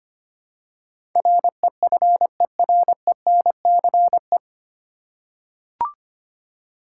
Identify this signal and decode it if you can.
Morse code, heard as a keyed tone, about 25 wpm, 700 Hz. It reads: REFERENCE
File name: practice / 25wpm / 9 Letter Words - First 3 Letters Given - 25wpm